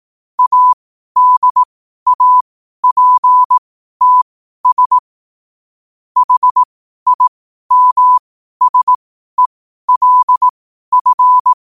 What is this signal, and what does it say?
Morse code, heard as a keyed tone, about 18 wpm, spaced as for 12 wpm, 1000 Hz. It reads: ADAPTS HIMSELF
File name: qotdfiles / 12wpm